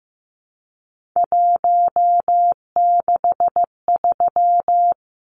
Morse code, heard as a keyed tone, about 15 wpm, 700 Hz. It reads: 163
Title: Numbers - 3-Digit 15wpm